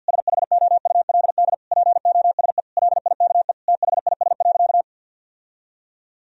Morse code, heard as a keyed tone, about 51 wpm, 700 Hz. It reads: SHOULD POSE LIKE THIS.
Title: Sentences from Top 1000 Words - Review Part 2 - 50wpm